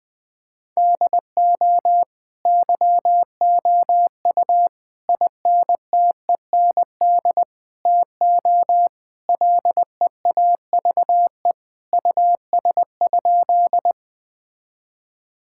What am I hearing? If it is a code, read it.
Morse code, heard as a keyed tone, about 20 wpm, 700 Hz. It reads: DO YOU INTEND TO LEAVE US?
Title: Sentences from Top 1600 Words 20wpm